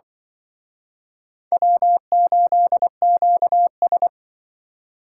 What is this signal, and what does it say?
Morse code, heard as a keyed tone, about 25 wpm, 690 Hz. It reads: W8QS
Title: Speed Racing - USA 1x2 Call Signs 20wpm